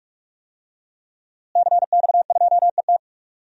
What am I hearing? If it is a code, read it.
Morse code, heard as a keyed tone, about 44 wpm, 700 Hz. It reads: CX2ET
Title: DXCC-100 Call Signs - 5 Characters - Repeated 3x - 45wpm